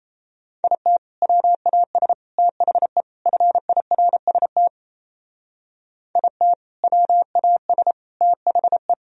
Morse code, heard as a keyed tone, about 36 wpm, 700 Hz. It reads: IT WAS THE FIRST IT WAS THE